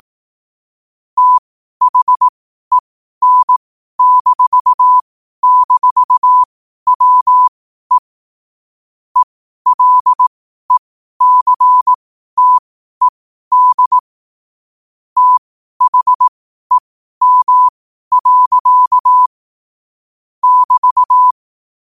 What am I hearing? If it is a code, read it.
Morse code, heard as a keyed tone, about 18 wpm, spaced as for 12 wpm, 1000 Hz. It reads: THEN--WE ELECTED THEM. <BT>
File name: qotdfiles / 12wpm